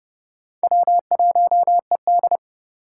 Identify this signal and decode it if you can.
Morse code, heard as a keyed tone, about 31 wpm, 700 Hz. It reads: W1ED
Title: USA Call Signs - 4 Characters - 2x Character Spacing - 30wpm